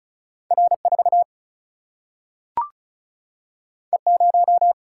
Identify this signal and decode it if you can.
Morse code, heard as a keyed tone, about 36 wpm, 700 Hz. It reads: R 4 E 0
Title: Single-Letter Single-Number 35wpm